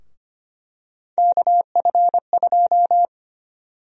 Morse code, heard as a keyed tone, about 25 wpm, 700 Hz. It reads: KF2
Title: USA Call Sign Prefixes - Repeated 2x - 25wpm